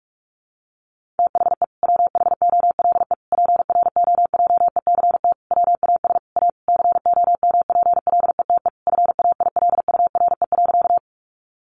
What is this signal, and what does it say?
Morse code, heard as a keyed tone, about 44 wpm, 700 Hz. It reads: THE WHOLE PROJECT WAS A COMPLETE FAILURE.
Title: Sentences from Top 1400 Words 45wpm